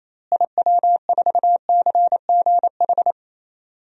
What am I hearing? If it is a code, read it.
Morse code, heard as a keyed tone, about 28 wpm, 700 Hz. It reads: IW4CGH
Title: DXCC-100 Call Signs - 6 Characters - Repeated 3x - 28wpm